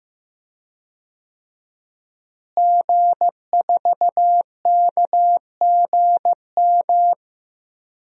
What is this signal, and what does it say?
Morse code, heard as a keyed tone, about 15 wpm, 700 Hz. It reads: G4KGM
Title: DXCC-100 Call Signs - 5 Characters - Repeated 2x - 15wpm